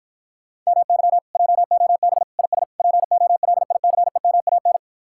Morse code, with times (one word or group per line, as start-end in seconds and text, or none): 0.7-1.2 MY
1.4-2.2 JOB
2.4-2.6 IS
2.8-4.8 POLICEMAN